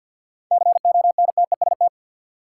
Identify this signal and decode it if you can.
Morse code, heard as a keyed tone, about 53 wpm, 700 Hz. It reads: CONTEST
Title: Sentences from Top 1600 Words - New Word Drill Alt - 50wpm